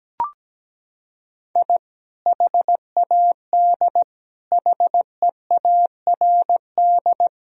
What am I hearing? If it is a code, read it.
Morse code, heard as a keyed tone, about 17 wpm, 700 Hz. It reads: I HAD HEARD
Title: Sentences from Top 1000 Words - Review Part 1 - 17wpm